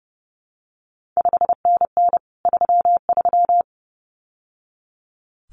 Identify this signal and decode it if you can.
Morse code, heard as a keyed tone, about 30 wpm, 700 Hz. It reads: 5NN 33
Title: CQ WW DX Contest Exchange Encore - 30wpm